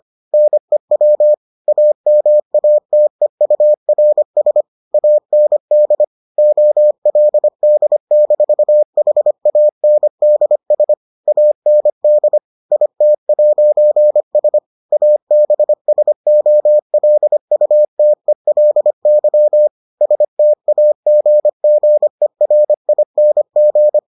Morse code, heard as a keyed tone, about 25 wpm, 600 Hz. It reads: NEW AMATEURS AND OLD-HANDS AND IT'S ABSOLUTELY STAGGERING